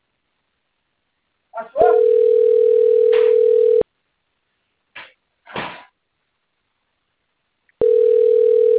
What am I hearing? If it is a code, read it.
Ringback tone (North America), 440 + 480 Hz.